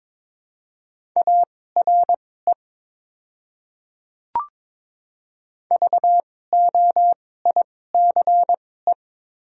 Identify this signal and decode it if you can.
Morse code, heard as a keyed tone, about 21 wpm, spaced as for 16 wpm, 690 Hz. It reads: ARE VOICE
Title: 2x Character Spacing - Top 500 Words - 22wpm